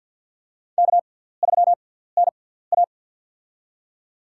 Morse code, heard as a keyed tone, about 48 wpm, spaced as for 17 wpm, 700 Hz. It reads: K3NA
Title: USA Call Signs - 4 Characters - 6x Character Spacing - 50wpm